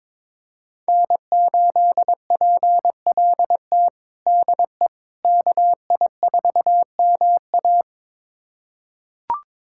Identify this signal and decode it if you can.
Morse code, heard as a keyed tone, about 21 wpm, 700 Hz. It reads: N8PLT DE KI4MA
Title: USA Callsign DE Callsign - 5 Characters - 22wpm